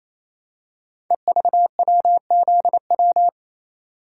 Morse code, heard as a keyed tone, about 28 wpm, 700 Hz. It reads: EVWZW
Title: Code Groups - 5 Letter:Numbers - 28wpm